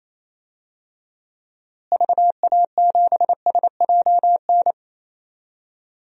Morse code, heard as a keyed tone, about 28 wpm, 700 Hz. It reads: VA7SJN
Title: DXCC-100 Call Signs - 6 Characters - Repeated 3x - 28wpm